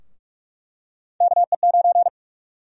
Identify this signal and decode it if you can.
Morse code, heard as a keyed tone, about 42 wpm, 700 Hz. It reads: KE9
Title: USA Call Sign Prefixes - 45wpm